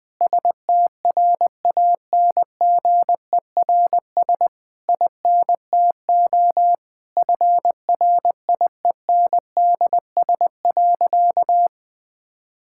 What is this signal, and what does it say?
Morse code, heard as a keyed tone, about 20 wpm, 700 Hz. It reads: STRANGERS INTO FRIENDS.
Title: Difficult Sentences with Hints - 20wpm